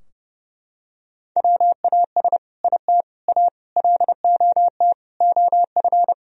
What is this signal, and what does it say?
Morse code, heard as a keyed tone, about 32 wpm, 700 Hz. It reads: WAS IT A LOT OF
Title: Sentences from Top 1000 Words - Review Part 4 - 30wpm